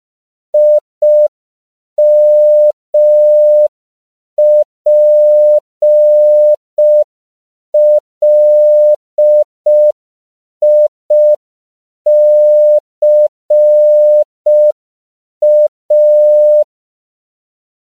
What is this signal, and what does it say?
Morse code, heard as a keyed tone, about 5 wpm, 600 Hz. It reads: IMPLICA
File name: CW / 5PPM